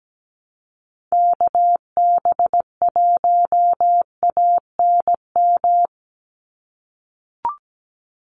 Morse code, heard as a keyed tone, about 17 wpm, 700 Hz. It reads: KB1ANM
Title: USA Call Signs 17wpm